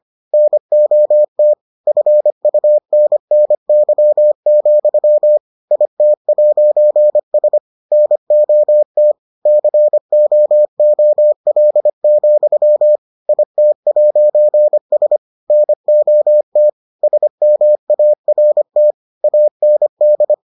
Morse code, heard as a keyed tone, about 25 wpm, 600 Hz. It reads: NOT FUNNY, IT'S NOT COOL, IT'S NOT SMART AND